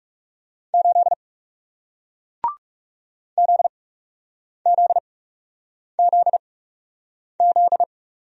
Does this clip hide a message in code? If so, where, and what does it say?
Yes: Morse code from 0.7 to 7.8 s, heard as a keyed tone, about 40 wpm, spaced as for 8 wpm, 700 Hz. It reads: 8 ZZZZ